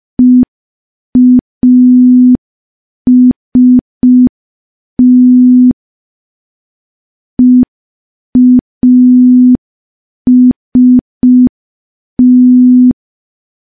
Morse code, heard as a keyed tone, about 5 wpm, 250 Hz. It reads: EAST EAST
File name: morse / morse-training-05WPM